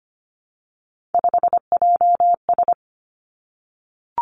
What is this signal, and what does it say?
Morse code, heard as a keyed tone, about 25 wpm, 700 Hz. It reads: 5JS
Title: Three Challenging Letter-Number Combinations 25wpm